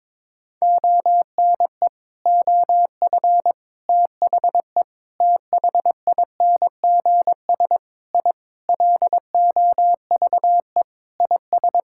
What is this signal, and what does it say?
Morse code, heard as a keyed tone, about 21 wpm, 700 Hz. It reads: ONE OF THE THINGS I LOVE IS
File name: practice / 22wpm / Sentences with 3-Word Phrases - 22wpm